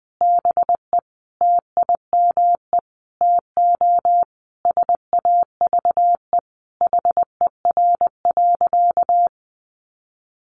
Morse code, heard as a keyed tone, about 20 wpm, 700 Hz. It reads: BE TIME TO SAVE HER.
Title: Sentences from Top 1100 Words 20wpm